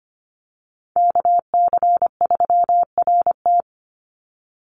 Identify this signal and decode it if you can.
Morse code, heard as a keyed tone, about 24 wpm, 700 Hz. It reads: KC3RT
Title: NA SOTA Chaser Callsigns 25wpm